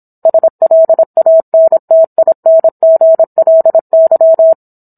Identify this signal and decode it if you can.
Morse code, heard as a keyed tone, about 27 wpm, 650 Hz. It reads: SLANTINGLY